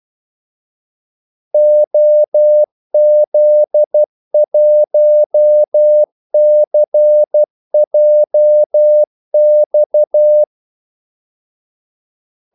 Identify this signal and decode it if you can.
Morse code, heard as a keyed tone, about 12 wpm, 600 Hz. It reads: OZ1CJX
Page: 2 ekstra ordmellemrum mellem kaldesignalerne